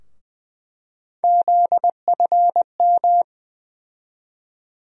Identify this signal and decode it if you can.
Morse code, heard as a keyed tone, about 20 wpm, 700 Hz. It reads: ZFM
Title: USA Call Sign Postfixes - 3 Letters 20wpm